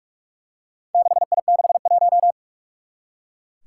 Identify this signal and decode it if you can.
Morse code, heard as a keyed tone, about 42 wpm, 700 Hz. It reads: 6I61
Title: Four Challenging Letter-Number Combinations 45wpm